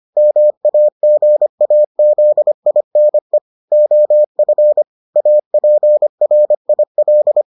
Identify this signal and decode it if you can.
Morse code, heard as a keyed tone, about 25 wpm, 600 Hz. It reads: MAGAZINE OF APRIL